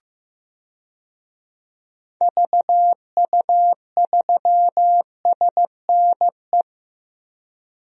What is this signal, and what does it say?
Morse code, heard as a keyed tone, about 15 wpm, 700 Hz. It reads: VU3SNE
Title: DXCC-100 Call Signs - 6 Characters - 15wpm